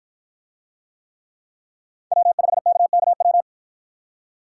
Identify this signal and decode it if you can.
Morse code, heard as a keyed tone, about 53 wpm, 700 Hz. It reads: W5KKW